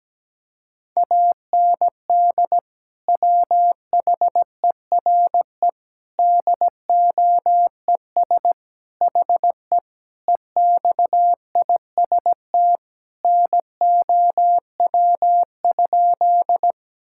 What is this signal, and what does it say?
Morse code, heard as a keyed tone, about 17 wpm, 700 Hz. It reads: AND WHERE DOES HE EXIST NOW?
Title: Sentences from Top 1000 Words - Review Part 5 - 17wpm